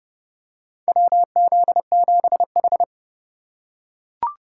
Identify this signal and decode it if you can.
Morse code, heard as a keyed tone, about 30 wpm, 700 Hz. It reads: WZ7H
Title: USA Call Signs - 4 Characters - 3x Character Spacing - 30wpm